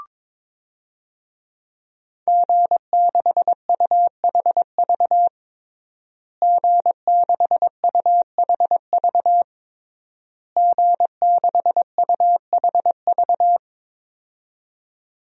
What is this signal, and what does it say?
Morse code, heard as a keyed tone, about 21 wpm, 700 Hz. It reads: G6UHV G6UHV G6UHV
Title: DXCC-100 Call Signs - 5 Characters - Repeated 3x - 22wpm